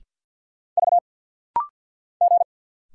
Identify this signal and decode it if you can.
Morse code, heard as a keyed tone, about 48 wpm, 700 Hz. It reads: V G
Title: Single Letter-Number - Rapid-Fire 50wpm